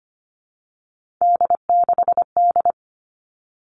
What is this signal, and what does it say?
Morse code, heard as a keyed tone, about 25 wpm, 700 Hz. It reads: D6D
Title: All Cut Numbers - 3-Digit 25wpm